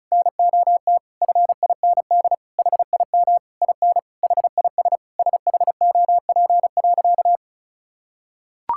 Morse code, heard as a keyed tone, about 36 wpm, 700 Hz. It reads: NOT FIND HIM IN HIS SHOP.
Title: Sentences from Top 1600 Words 35wpm